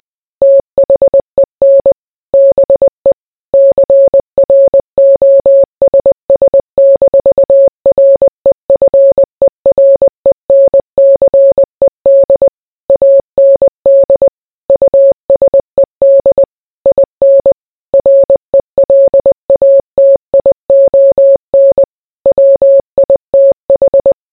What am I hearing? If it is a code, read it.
Morse code, heard as a keyed tone, about 20 wpm, 550 Hz. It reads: THEN BE CROSS-REFERENCED AND USED IN RELATION WITH